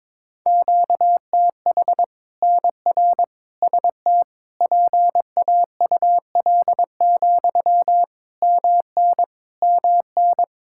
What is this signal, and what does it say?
Morse code, heard as a keyed tone, about 22 wpm, 700 Hz. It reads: QTH NR ST PAUL, MN MN
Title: Practice QSO - Exchange 1 - Beginning - 22wpm